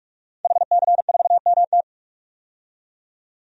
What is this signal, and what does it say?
Morse code, heard as a keyed tone, about 46 wpm, 700 Hz. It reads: HC4MT
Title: DXCC-100 Call Signs - 5 Characters - 45wpm